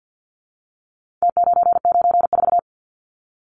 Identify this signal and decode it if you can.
Morse code, heard as a keyed tone, about 50 wpm, 700 Hz. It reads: T994
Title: Numbers - Contest Style - 4-Digit 50wpm